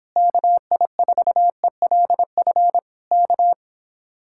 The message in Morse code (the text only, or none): KI4ELF K